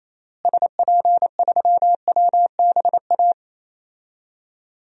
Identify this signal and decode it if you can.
Morse code, heard as a keyed tone, about 28 wpm, 700 Hz. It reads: SP3WBA
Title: DXCC-100 Call Signs - 6 Characters - Repeated 2x - 28wpm